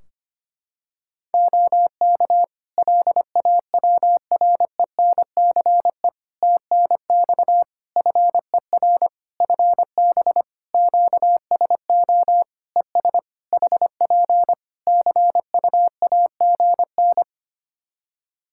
Morse code, heard as a keyed tone, about 25 wpm, 700 Hz. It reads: OK LAWRENCE TNX FER FB QSO ES HP CU AGN
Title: Practice QSO - Exchange 4 - Ending - 25wpm